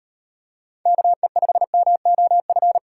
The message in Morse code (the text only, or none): KE5MOF